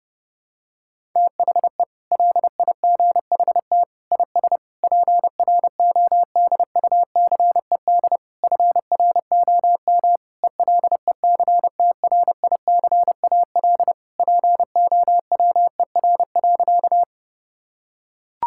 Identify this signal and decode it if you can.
Morse code, heard as a keyed tone, about 30 wpm, 700 Hz. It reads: THE LIGHT IS PRODUCED FROM ELECTRICAL POWER.
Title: Sentences from Top 1000 Words - Review Part 6 - 30wpm